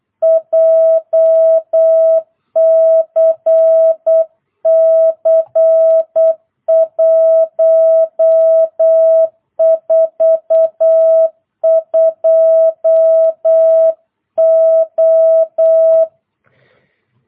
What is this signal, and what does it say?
Morse code, heard as a keyed tone, about 8 wpm, 650 Hz. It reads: JCC 142O